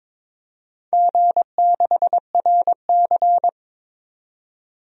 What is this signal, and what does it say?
Morse code, heard as a keyed tone, about 22 wpm, 700 Hz. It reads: G6RC
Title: DXCC-100 Call Signs - 4 Characters - 22wpm